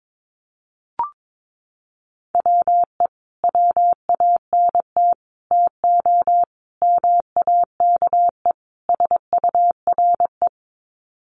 Morse code, heard as a keyed tone, about 22 wpm, 700 Hz. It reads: WE WANT TO MAKE SURE
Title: Sets of 5 Words Encore2 22wpm